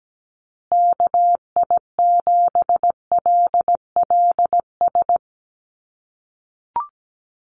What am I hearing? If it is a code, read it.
Morse code, heard as a keyed tone, about 17 wpm, 700 Hz. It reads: KI7LLS
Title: USA Call Signs Repeated 3x 17wpm